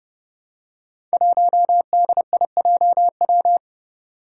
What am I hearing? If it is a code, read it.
Morse code, heard as a keyed tone, about 28 wpm, 700 Hz. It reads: 1DIJW
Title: Five Challenging Letter-Number Combinations 30wpm